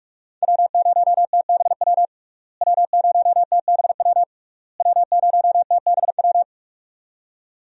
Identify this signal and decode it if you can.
Morse code, heard as a keyed tone, about 46 wpm, 700 Hz. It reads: W0TBW W0TBW W0TBW
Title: USA Call Signs - 5 Characters - Repeated 3x - 45wpm